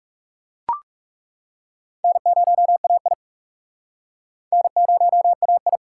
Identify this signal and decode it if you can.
Morse code, heard as a keyed tone, about 36 wpm, 700 Hz. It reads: N0AI N0AI
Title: Speed Racing - USA 1x2 Call Signs 30wpm